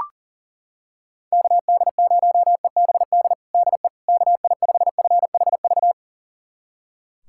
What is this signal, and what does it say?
Morse code, heard as a keyed tone, about 40 wpm, 690 Hz. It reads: KD0EBD DE KI5FHV
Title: USA Callsign DE Callsign - 6 Characters - 40wpm